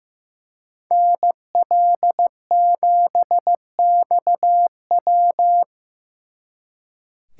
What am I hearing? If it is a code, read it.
Morse code, heard as a keyed tone, about 18 wpm, 700 Hz. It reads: NL7XW
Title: Speed Racing - USA 2x2 Call Signs 15wpm